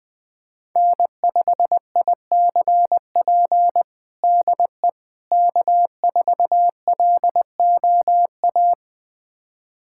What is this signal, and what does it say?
Morse code, heard as a keyed tone, about 20 wpm, 700 Hz. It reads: N5ICP DE K4LOA
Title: USA Callsign DE Callsign - 5 Characters - 20wpm